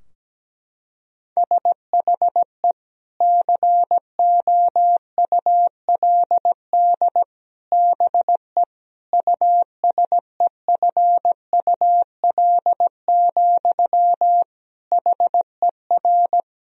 Morse code, heard as a keyed tone, about 17 wpm, 700 Hz. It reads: SHE COULD BE USEFUL, HER
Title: Sentences from Top 1000 Words - Review Part 4 - 17wpm